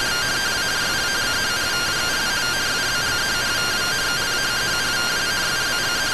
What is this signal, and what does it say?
Two-tone teleprinter signal, about 170 Hz shift.